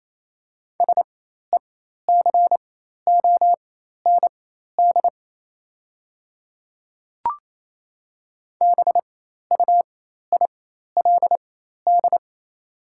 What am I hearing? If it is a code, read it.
Morse code, heard as a keyed tone, about 28 wpm, spaced as for 13 wpm, 690 Hz. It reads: SECOND BUILD
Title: 4x Character Spacing - Top 300 Words - 28wpm